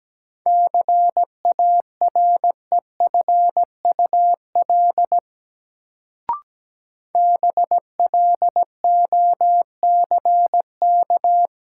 Morse code, heard as a keyed tone, about 17 wpm, 700 Hz. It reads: CAREFUL BLOCK
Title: Sentences from Top 1600 Words - New Word Drill Alt - 17wpm